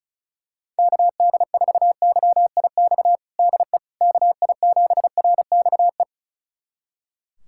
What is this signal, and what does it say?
Morse code, heard as a keyed tone, about 34 wpm, 700 Hz. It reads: KD4YIX DE KI7RXE
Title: USA Callsign DE Callsign - 6 Characters - 35wpm